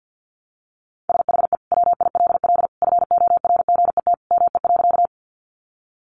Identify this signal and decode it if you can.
Morse code, heard as a keyed tone, about 50 wpm, 700 Hz. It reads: SHE WILL FORGET ME.
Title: Sentences from Top 1200 Words 50wpm